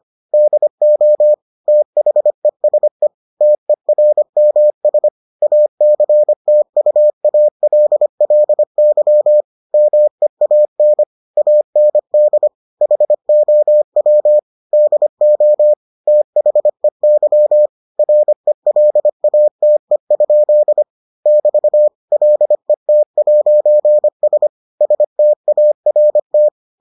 Morse code, heard as a keyed tone, about 25 wpm, 600 Hz. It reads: DO THESE TERMS ACTUALLY MEAN AND HOW DO THEY RELATE? <BT> LET'S START